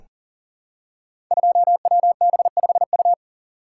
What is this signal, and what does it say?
Morse code, heard as a keyed tone, about 42 wpm, 700 Hz. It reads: 2WB5U
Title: Code Groups - 5 Letter:Numbers - 40wpm